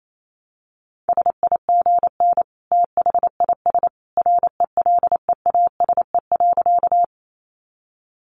Morse code, heard as a keyed tone, about 28 wpm, 700 Hz. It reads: SIGN THIS RELEASE.
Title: A Sound Of Thunder 28wpm